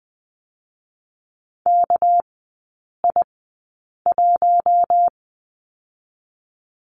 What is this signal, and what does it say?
Morse code, heard as a keyed tone, about 20 wpm, spaced as for 8 wpm, 700 Hz. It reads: KI1
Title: 198 Lesson 32e - Letter k - Three Characters 20wpm